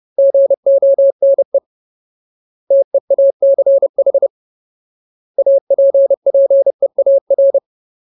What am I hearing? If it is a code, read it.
Morse code, heard as a keyed tone, about 30 wpm, 550 Hz. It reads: GONE TEACH APPEAR
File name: Ultimate Word List / 30WPM Ultimate